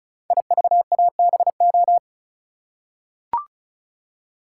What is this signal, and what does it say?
Morse code, heard as a keyed tone, about 36 wpm, 700 Hz. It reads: IVABO
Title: Code Groups - 5 Letters - 35wpm